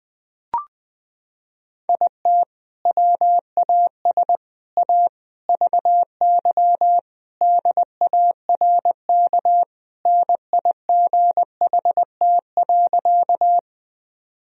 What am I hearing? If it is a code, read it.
Morse code, heard as a keyed tone, about 20 wpm, 700 Hz. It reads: IT WAS A VY DARK NIGHT.
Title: Sentences from Top 500 Words and Abbreviations 20wpm